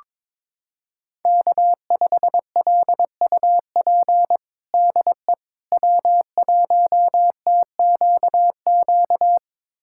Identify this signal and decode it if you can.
Morse code, heard as a keyed tone, about 22 wpm, 700 Hz. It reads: K5LUP DE W1TQQ
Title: USA Callsign DE Callsign - 5 Characters - 22wpm